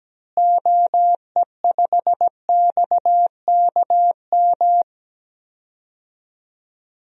Morse code, heard as a keyed tone, about 17 wpm, 700 Hz. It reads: OE5XKM
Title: DXCC-100 Call Signs - 6 Characters - Repeated 3x - 17wpm